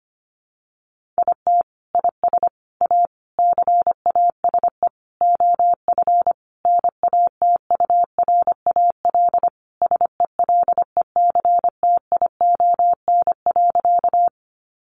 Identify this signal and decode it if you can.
Morse code, heard as a keyed tone, about 25 wpm, 700 Hz. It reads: IT IS A CASE OF NATURAL SELECTION.
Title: Sentences from Top 1400 Words 25wpm